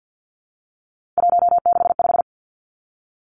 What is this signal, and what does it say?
Morse code, heard as a keyed tone, about 50 wpm, 700 Hz. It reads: 165
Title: Three Challenging Letter-Number Combinations 50wpm